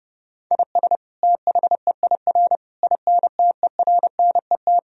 Morse code, heard as a keyed tone, about 30 wpm, 700 Hz. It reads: IS THEIR INTERNET
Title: Sentences from Top 1000 Words - Review Part 4 - 30wpm